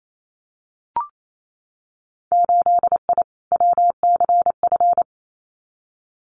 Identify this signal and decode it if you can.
Morse code, heard as a keyed tone, about 28 wpm, 700 Hz. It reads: 8I WCF
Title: Winter Field Day Exchanges Encore 28wpm